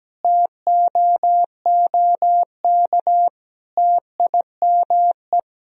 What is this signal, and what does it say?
Morse code, heard as a keyed tone, about 17 wpm, 700 Hz. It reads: TOOK TIME
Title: Sentences from Top 1000 Words - Review Part 4 - 17wpm